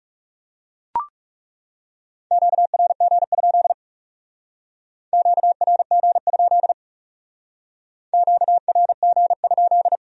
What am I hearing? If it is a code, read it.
Morse code, heard as a keyed tone, about 36 wpm, 700 Hz. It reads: QRG? QRG? QRG?
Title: Speed-Racing Q-Codes - 30wpm